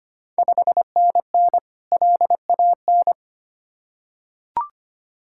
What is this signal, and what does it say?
Morse code, heard as a keyed tone, about 25 wpm, 700 Hz. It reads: 5NN LAN
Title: Ontario QSO Party Exchange 25wpm